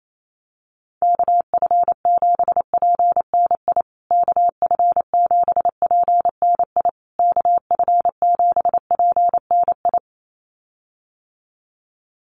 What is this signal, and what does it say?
Morse code, heard as a keyed tone, about 28 wpm, 700 Hz. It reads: KF7PNI KF7PNI KF7PNI
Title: USA Call Signs Repeated 3x 28wpm